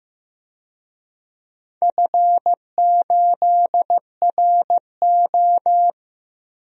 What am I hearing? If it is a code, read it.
Morse code, heard as a keyed tone, about 15 wpm, 700 Hz. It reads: F8RO
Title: DXCC-100 Call Signs - 4 Characters - Repeated 3x - 15wpm